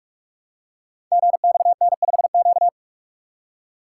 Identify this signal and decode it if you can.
Morse code, heard as a keyed tone, about 44 wpm, 700 Hz. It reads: GXN5Q
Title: Code Groups - 5 Letter:Numbers - 45wpm